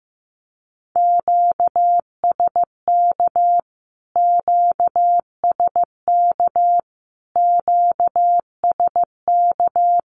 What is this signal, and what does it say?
Morse code, heard as a keyed tone, about 15 wpm, 700 Hz. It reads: QSK QSK QSK
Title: Q-Codes Repeated 3x 15wpm